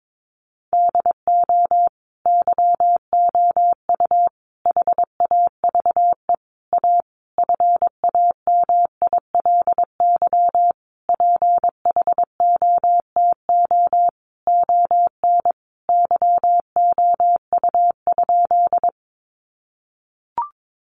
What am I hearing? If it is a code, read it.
Morse code, heard as a keyed tone, about 22 wpm, 700 Hz. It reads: DO YOU HAVE A FAMILY PHOTO ON YOU?
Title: Sentences from Top 1200 Words 22wpm